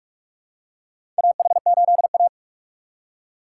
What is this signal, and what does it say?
Morse code, heard as a keyed tone, about 46 wpm, 700 Hz. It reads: AH8A
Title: USA Call Signs - 3x Character Spacing - 45wpm